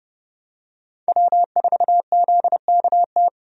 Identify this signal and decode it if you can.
Morse code, heard as a keyed tone, about 30 wpm, 700 Hz. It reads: W4ZKT
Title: USA Call Signs - 5 Characters - 30wpm